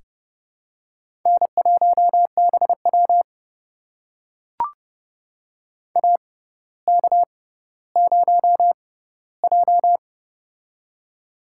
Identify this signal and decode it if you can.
Morse code, heard as a keyed tone, about 29 wpm, spaced as for 15 wpm, 700 Hz. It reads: N1BW AK0J
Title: USA Call Signs - 4 Characters - 6x Character Spacing - 30wpm